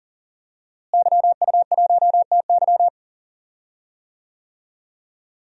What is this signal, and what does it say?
Morse code, heard as a keyed tone, about 40 wpm, 700 Hz. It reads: YU1TY
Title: DXCC-100 Call Signs - 5 Characters - Repeated 2x - 40wpm